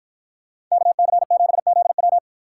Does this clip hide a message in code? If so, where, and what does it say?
Yes: Morse code from 0.7 to 2.2 s, heard as a keyed tone, about 53 wpm, 700 Hz. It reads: KC7ZW